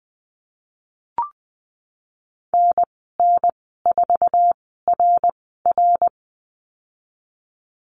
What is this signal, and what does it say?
Morse code, heard as a keyed tone, about 19 wpm, spaced as for 14 wpm, 700 Hz. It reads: NN4RR
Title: 062 Lesson 11g - Letter h - Call Signs 20wpm